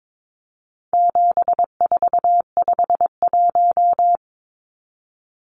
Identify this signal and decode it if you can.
Morse code, heard as a keyed tone, about 22 wpm, 700 Hz. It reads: 7451
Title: Numbers - 4-Digit 22wpm